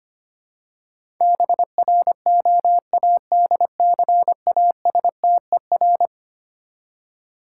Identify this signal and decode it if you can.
Morse code, heard as a keyed tone, about 25 wpm, 700 Hz. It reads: BROADCASTER
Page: morse-code-ninja-practice-25wpm